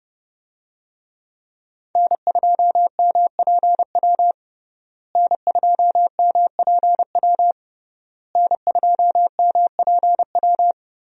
Morse code, heard as a keyed tone, about 30 wpm, 700 Hz. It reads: N2MPW N2MPW N2MPW
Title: USA Call Signs - 5 Characters - Repeated 3x - 30wpm